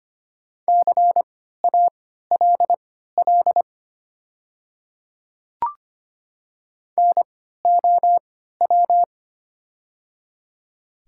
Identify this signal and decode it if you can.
Morse code, heard as a keyed tone, about 25 wpm, spaced as for 14 wpm, 690 Hz. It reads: CALL NOW
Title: 3x Character Spacing - Top 100 Words - 25wpm